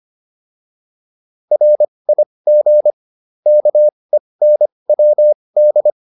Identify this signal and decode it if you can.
Morse code, heard as a keyed tone, about 25 wpm, spaced as for 20 wpm, 600 Hz. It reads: RIG KENWD
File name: QSO308a 20